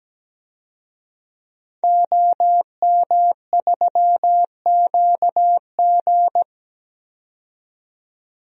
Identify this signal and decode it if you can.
Morse code, heard as a keyed tone, about 17 wpm, 700 Hz. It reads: OM3QG
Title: DXCC-100 Call Signs - 5 Characters - 17wpm